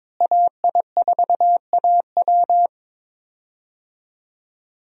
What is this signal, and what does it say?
Morse code, heard as a keyed tone, about 22 wpm, 700 Hz. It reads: AI4AW
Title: USA Call Signs - 5 Characters - Repeated 3x - 22wpm